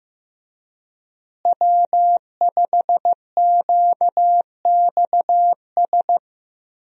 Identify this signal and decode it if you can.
Morse code, heard as a keyed tone, about 15 wpm, 700 Hz. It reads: W5QXS
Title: USA Call Signs - 5 Characters - Repeated 3x - 15wpm